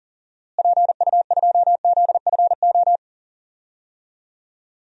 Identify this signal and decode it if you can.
Morse code, heard as a keyed tone, about 40 wpm, 700 Hz. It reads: PU2ZFO
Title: DXCC-100 Call Signs - 6 Characters - Repeated 2x - 40wpm